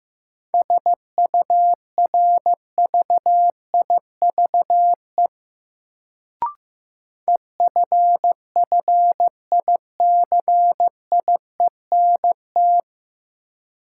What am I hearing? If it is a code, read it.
Morse code, heard as a keyed tone, about 15 wpm, 700 Hz. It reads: SURVIVE EFFICIENT
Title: Sentences from Top 1300 Words - New Word Drill Alt - 15wpm